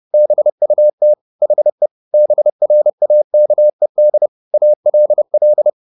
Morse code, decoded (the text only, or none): BUT HE BRAKED ALL